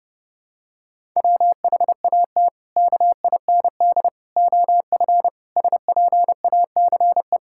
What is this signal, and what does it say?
Morse code, heard as a keyed tone, about 30 wpm, 700 Hz. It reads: WHAT KIND OF SPACE
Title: Sentences from Top 1000 Words - Review Part 5 - 30wpm